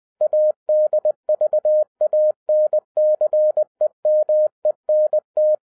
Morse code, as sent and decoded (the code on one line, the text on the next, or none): .- -.. ...- .- -. -.-. -- . -. -
ADVANCEMENT